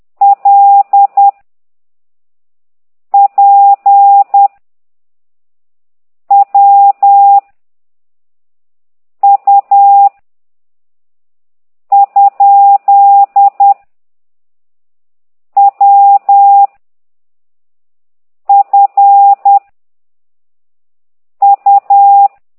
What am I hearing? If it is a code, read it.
Morse code, heard as a keyed tone, about 10 wpm, 800 Hz. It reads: L P W U ? W F U
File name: ct2idw / exerc9